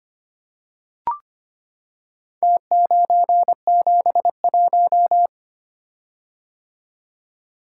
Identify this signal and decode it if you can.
Morse code, heard as a keyed tone, about 25 wpm, 700 Hz. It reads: T971
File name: practice / 25wpm / Four Letter-Number Combinations - 25wpm